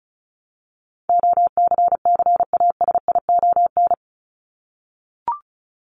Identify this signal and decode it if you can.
Morse code, heard as a keyed tone, about 34 wpm, 700 Hz. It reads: OCCASION
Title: Top 1501-1750 Words 35wpm